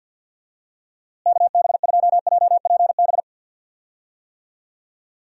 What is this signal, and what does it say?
Morse code, heard as a keyed tone, about 51 wpm, 700 Hz. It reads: KB2JPB
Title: USA Call Signs - 6 Characters - Repeated 3x - 50wpm